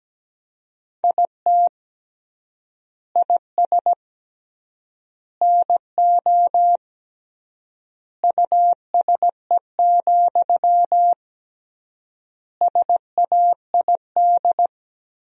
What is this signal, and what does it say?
Morse code, heard as a keyed tone, about 17 wpm, 700 Hz. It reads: IT IS NO USE, SAID
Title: 3x Word-Spacing - Sentences from Top 500 Words - 17wpm